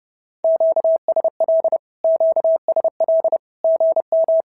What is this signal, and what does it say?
Morse code, heard as a keyed tone, about 30 wpm, 650 Hz. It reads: QSL QSL GM
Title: SOTA QSO Practice 30wpm